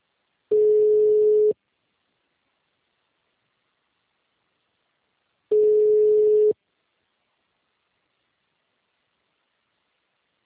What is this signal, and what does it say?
Ringback tone (Europe, 425 Hz).